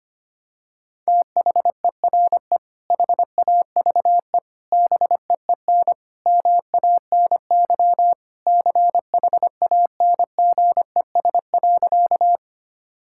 Morse code, heard as a keyed tone, about 25 wpm, 700 Hz. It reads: THERE HAVE BEEN MANY CHANGES.
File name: practice / 25wpm / Sentences with 3-Word Phrases - 25wpm